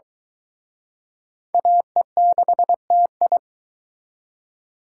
Morse code, heard as a keyed tone, about 23 wpm, 700 Hz. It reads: AE6TI